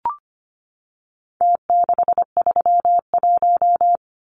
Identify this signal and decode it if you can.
Morse code, heard as a keyed tone, about 25 wpm, 700 Hz. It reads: T631
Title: Numbers - Contest Style - 4-Digit 25wpm